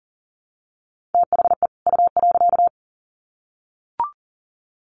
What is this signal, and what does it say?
Morse code, heard as a keyed tone, about 40 wpm, 700 Hz. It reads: THE U.
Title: Sets of 2 Words 40wpm